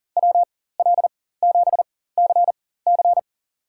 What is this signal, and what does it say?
Morse code, heard as a keyed tone, about 40 wpm, spaced as for 19 wpm, 700 Hz. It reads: WL7CC
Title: USA Call Signs - 5 Characters - 4x Character Spacing - 40wpm